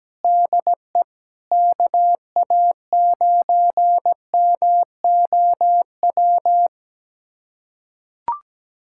Morse code, heard as a keyed tone, about 17 wpm, 700 Hz. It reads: DE KA9MOW
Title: USA Callsign DE Callsign - 6 Characters - 17wpm